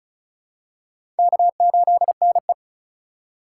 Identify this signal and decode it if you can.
Morse code, heard as a keyed tone, about 34 wpm, 700 Hz. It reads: K8NE
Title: USA Call Signs - 4 Characters - 4x Character Spacing - 35wpm